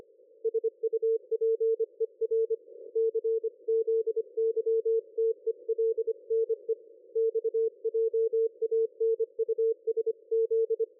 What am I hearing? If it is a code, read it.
Morse code, heard as a keyed tone, about 24 wpm, 450 Hz. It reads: SUPER CZYTELNE X JANUSZ
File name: dlazaawansowanych / nagraniazpasma / ExpertSDR2 line out 23.01.2020 14.04.01